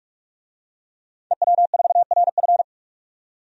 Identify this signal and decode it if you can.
Morse code, heard as a keyed tone, about 42 wpm, 700 Hz. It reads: EW4RF
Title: DXCC-100 Call Signs - 5 Characters - 45wpm